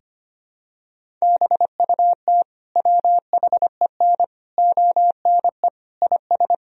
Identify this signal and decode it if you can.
Morse code, heard as a keyed tone, about 25 wpm, 700 Hz. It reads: BUT WHEN ONE IS
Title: Sentences from Top 1000 Words - Review Part 1 - 25wpm